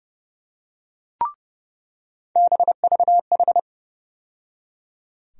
Morse code, heard as a keyed tone, about 30 wpm, 700 Hz. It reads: BVH
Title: USA Call Sign Postfixes - 3 Letters 30wpm